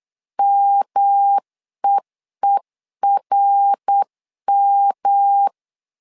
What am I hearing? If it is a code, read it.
Morse code, heard as a keyed tone, about 8 wpm, 800 Hz. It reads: MEERM
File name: LEZIONI / 3 MWRO / 3 MWR 8wpm 2444